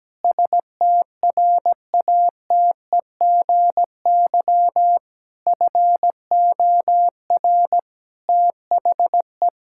Morse code, heard as a keyed tone, about 17 wpm, 700 Hz. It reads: STRATEGY FOR THE
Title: Sentences from Top 1000 Words - Review Part 5 - 17wpm